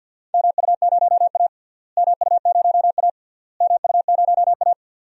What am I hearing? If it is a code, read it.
Morse code, heard as a keyed tone, about 49 wpm, 700 Hz. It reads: MU0A MU0A MU0A
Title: DXCC-100 Call Signs - 4 Characters - Repeated 3x - 50wpm